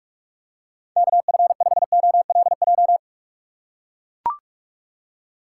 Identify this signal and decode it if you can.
Morse code, heard as a keyed tone, about 44 wpm, 700 Hz. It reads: KF5OLJ
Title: USA Call Signs - 6 Characters - 6x Character Spacing - 45wpm